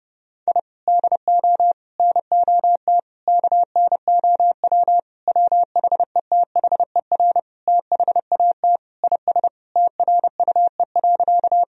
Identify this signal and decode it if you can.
Morse code, heard as a keyed tone, about 30 wpm, 700 Hz. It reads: I DO NOT KNOW WHETHER THAT IS TRUE.
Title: Sentences from Top 1000 Words - Review Part 3 - 30wpm